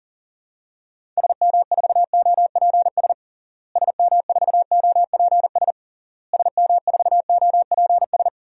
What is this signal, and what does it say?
Morse code, heard as a keyed tone, about 40 wpm, 700 Hz. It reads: SM4OPS SM4OPS SM4OPS
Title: DXCC-100 Call Signs - 6 Characters - Repeated 3x - 40wpm